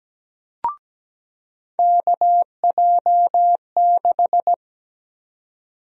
Morse code, heard as a keyed tone, about 17 wpm, 700 Hz. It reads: KJ6
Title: USA Call Sign Prefixes - 17wpm